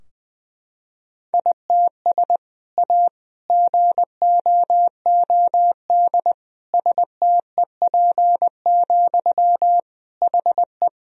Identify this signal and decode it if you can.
Morse code, heard as a keyed tone, about 20 wpm, 700 Hz. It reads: ITS A GOOD STEP, HE
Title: Sentences from Top 1000 Words - Review Part 2 - 20wpm